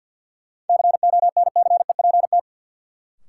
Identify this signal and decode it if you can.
Morse code, heard as a keyed tone, about 52 wpm, 700 Hz. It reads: CONCEPT
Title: Sentences from Top 1100 Words - New Word Drill - 50wpm